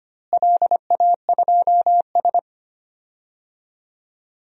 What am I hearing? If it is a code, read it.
Morse code, heard as a keyed tone, about 25 wpm, 700 Hz. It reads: LA2S
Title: DXCC-100 Call Signs - 4 Characters - Repeated 3x - 25wpm